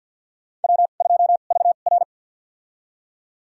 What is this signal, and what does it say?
Morse code, heard as a keyed tone, about 48 wpm, spaced as for 35 wpm, 700 Hz. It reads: W2VR
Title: USA Call Signs - 4 Characters - 2x Character Spacing - 50wpm